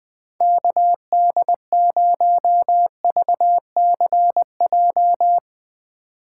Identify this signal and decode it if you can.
Morse code, heard as a keyed tone, about 20 wpm, 700 Hz. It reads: KD0VCJ